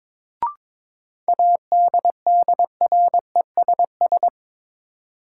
Morse code, heard as a keyed tone, about 22 wpm, 700 Hz. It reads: ADDRESS
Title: Sentences from Top 1000 Words - New Word Drill - 22wpm